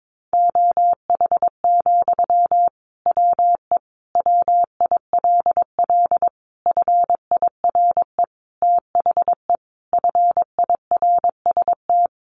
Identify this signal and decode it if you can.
Morse code, heard as a keyed tone, about 22 wpm, 700 Hz. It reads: OH, WE WILL FIRE THE FIRST
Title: Sentences from Top 1100 Words 22wpm